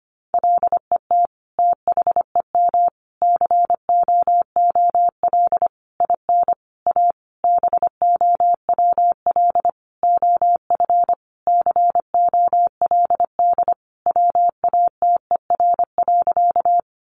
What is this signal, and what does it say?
Morse code, heard as a keyed tone, about 25 wpm, 700 Hz. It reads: LET THEM COOL IN A BOWL OF COLD WATER.
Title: Sentences from Top 1200 Words 25wpm